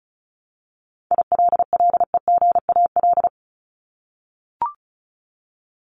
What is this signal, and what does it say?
Morse code, heard as a keyed tone, about 34 wpm, 700 Hz. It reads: ILLEGAL
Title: Top 1751-2000 Words 35wpm